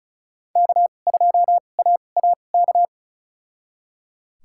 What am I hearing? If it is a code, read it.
Morse code, heard as a keyed tone, about 36 wpm, spaced as for 25 wpm, 700 Hz. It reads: K2AAK
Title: USA Call Signs - 5 Characters - 2x Character Spacing - 35wpm